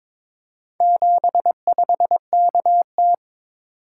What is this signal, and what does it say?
Morse code, heard as a keyed tone, about 22 wpm, 700 Hz. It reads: 75KT
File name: practice / 22wpm / Four Letter-Number Combinations - 22wpm